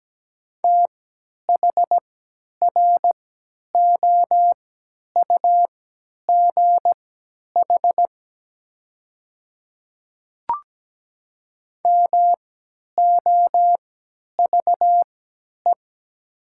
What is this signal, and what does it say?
Morse code, heard as a keyed tone, about 17 wpm, spaced as for 10 wpm, 700 Hz. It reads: THROUGH MOVE
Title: 3x Character Spacing - Top 300 Words - 17wpm